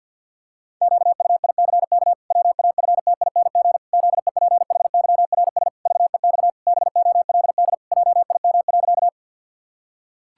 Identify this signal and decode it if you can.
Morse code, heard as a keyed tone, about 50 wpm, 700 Hz. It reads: QUICK WAFTING ZEPHYRS VEX BOLD JIM.
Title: Pangram Challenge v2 - 50wpm